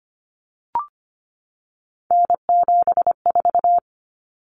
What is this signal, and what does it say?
Morse code, heard as a keyed tone, about 25 wpm, 700 Hz. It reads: N74
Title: Common Cut Numbers - 3-Digit 25wpm